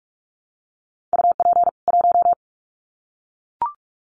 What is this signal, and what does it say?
Morse code, heard as a keyed tone, about 44 wpm, 700 Hz. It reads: UP 1
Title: Common QSO Phrases 45wpm